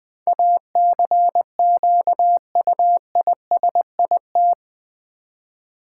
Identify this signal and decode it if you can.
Morse code, heard as a keyed tone, about 21 wpm, 700 Hz. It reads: ACQUISIT
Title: Build A Word - 4 Letters At A Time - 11-Letter Words - 20wpm